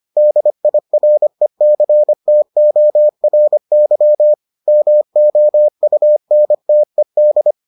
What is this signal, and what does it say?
Morse code, heard as a keyed tone, about 25 wpm, 600 Hz. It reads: DIRECTORY MOUNTED